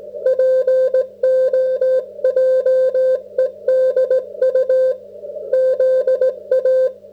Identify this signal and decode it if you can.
Morse code, heard as a keyed tone, about 17 wpm, 530 Hz. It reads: POJEDU ZA